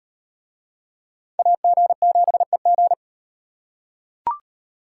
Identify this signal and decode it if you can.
Morse code, heard as a keyed tone, about 38 wpm, 700 Hz. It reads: AG7EG